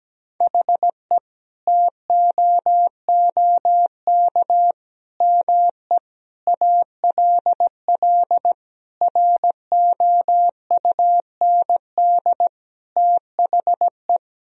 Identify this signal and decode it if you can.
Morse code, heard as a keyed tone, about 17 wpm, 700 Hz. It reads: HE TOOK ME ALL ROUND THE
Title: Sentences from Top 1000 Words - Review Part 1 - 17wpm